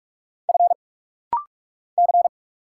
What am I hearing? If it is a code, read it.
Morse code, heard as a keyed tone, about 46 wpm, 690 Hz. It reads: F C
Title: Single Letters - Rapid-Fire 45wpm